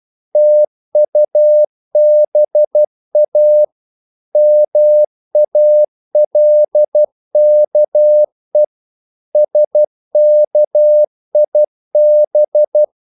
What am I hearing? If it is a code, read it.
Morse code, heard as a keyed tone, about 12 wpm, 600 Hz. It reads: TUBA MALKE SKIB
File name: Korte ord DK 12wpm